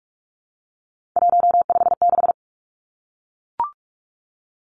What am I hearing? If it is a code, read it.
Morse code, heard as a keyed tone, about 44 wpm, 700 Hz. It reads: 156
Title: Numbers - 3-Digit 45wpm